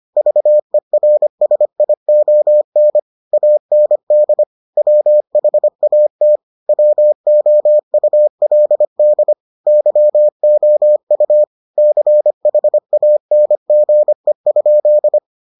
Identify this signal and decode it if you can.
Morse code, heard as a keyed tone, about 25 wpm, 600 Hz. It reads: VERSION AND WHAT WOULD YOU CHANGE?